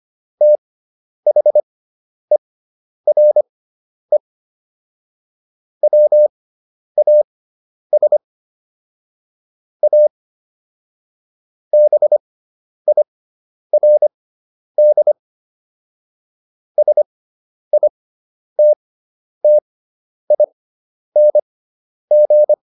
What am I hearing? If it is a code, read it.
Morse code, heard as a keyed tone, about 25 wpm, spaced as for 10 wpm, 600 Hz. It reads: THERE WAS A BIRD SITTING